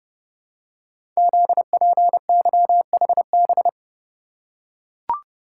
Morse code, heard as a keyed tone, about 30 wpm, 700 Hz. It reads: ZPYHB
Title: Code Groups - 5 Letter:Numbers - 30wpm